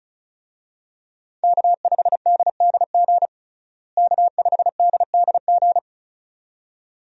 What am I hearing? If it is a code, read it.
Morse code, heard as a keyed tone, about 34 wpm, 700 Hz. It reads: K5DDG K5DDG
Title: USA Call Signs - 5 Characters - Repeated 2x - 35wpm